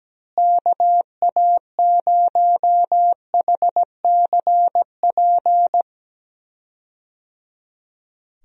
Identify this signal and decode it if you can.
Morse code, heard as a keyed tone, about 17 wpm, 700 Hz. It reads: KA0HCP
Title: USA Call Signs - 6 Characters - Repeated 3x - 17wpm